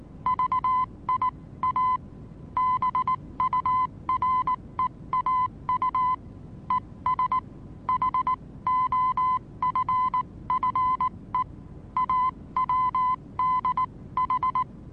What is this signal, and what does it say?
Morse code, heard as a keyed tone, about 19 wpm, spaced as for 16 wpm, 1020 Hz. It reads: VIA BUREAU ES HOFFE AWDH
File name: repos / Morsekurs / DJ4UF Morsen26